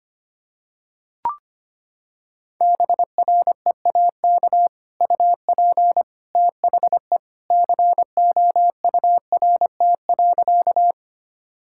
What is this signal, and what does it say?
Morse code, heard as a keyed tone, about 25 wpm, 700 Hz. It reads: BREAK UP THE COURT.
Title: Sentences from Top 1600 Words 25wpm